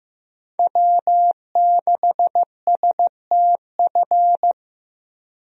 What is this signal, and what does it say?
Morse code, heard as a keyed tone, about 16 wpm, 700 Hz. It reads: W6STF
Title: USA Call Signs - 5 Characters - Repeated 2x - 15wpm